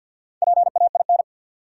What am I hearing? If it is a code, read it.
Morse code, heard as a keyed tone, about 47 wpm, 700 Hz. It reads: PAIN
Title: Sentences from Top 1600 Words - New Word Drill - 50wpm